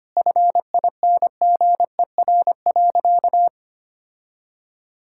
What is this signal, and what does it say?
Morse code, heard as a keyed tone, about 25 wpm, 700 Hz. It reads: FINGER.
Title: Sentences from Top 1600 Words 25wpm